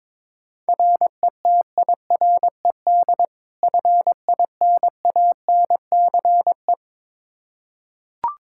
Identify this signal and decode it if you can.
Morse code, heard as a keyed tone, about 22 wpm, 700 Hz. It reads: RETIRED FINANCE
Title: Practice QSO - Exchange 3 - More Info - 22wpm